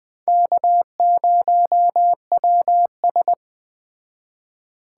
Morse code, heard as a keyed tone, about 21 wpm, 700 Hz. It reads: K0WS
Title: Speed Racing - USA 1x2 Call Signs 17wpm